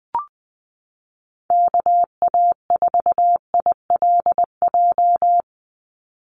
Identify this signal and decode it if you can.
Morse code, heard as a keyed tone, about 20 wpm, 700 Hz. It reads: KA4ILJ
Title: USA Call Signs 20wpm